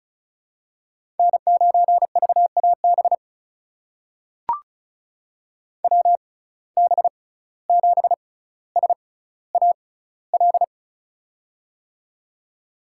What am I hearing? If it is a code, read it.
Morse code, heard as a keyed tone, about 36 wpm, spaced as for 17 wpm, 700 Hz. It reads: N9VAB WB7SAL
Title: USA Call Signs - 6x Character Spacing - 35wpm